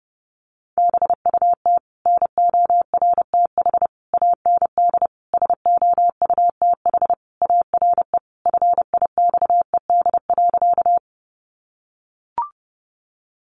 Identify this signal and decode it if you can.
Morse code, heard as a keyed tone, about 30 wpm, 700 Hz. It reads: BUT NORTH AND SOUTH ARE FIXED.
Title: Sentences from Top 1100 Words 30wpm